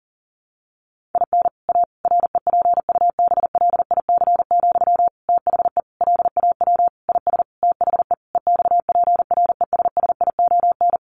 Morse code, heard as a keyed tone, about 40 wpm, 700 Hz. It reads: IN A REPUBLIC, THE LAW IS THE EXPRESSION
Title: Sentences from Top 1500 Words 40wpm